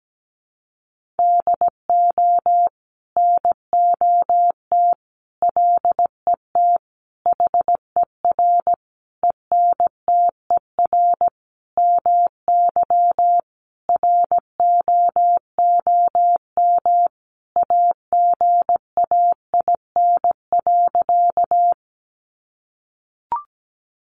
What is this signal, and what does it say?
Morse code, heard as a keyed tone, about 17 wpm, 700 Hz. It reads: DO NOT LET HER ENTER MY ROOM AGAIN.
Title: Sentences from Top 1300 Words 17wpm